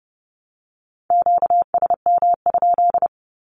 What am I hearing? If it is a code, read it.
Morse code, heard as a keyed tone, about 30 wpm, 700 Hz. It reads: QSM?
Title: Q-Codes 30wpm